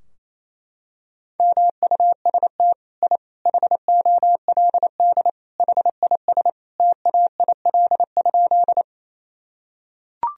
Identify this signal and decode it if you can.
Morse code, heard as a keyed tone, about 27 wpm, 700 Hz. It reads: MUST I HOLD HIS TAIL?
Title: Sentences from Top 1000 Words - Review Part 3 - 28wpm